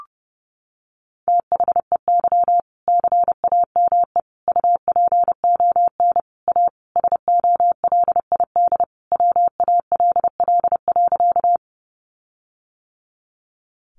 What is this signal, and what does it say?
Morse code, heard as a keyed tone, about 30 wpm, 700 Hz. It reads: THEY CAME UPON A SOLID WALL.
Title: Sentences from Top 1500 Words 30wpm